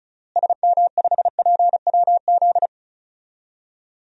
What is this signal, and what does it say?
Morse code, heard as a keyed tone, about 36 wpm, 700 Hz. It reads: SM5PWZ
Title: DXCC-100 Call Signs - 6 Characters - Repeated 3x - 35wpm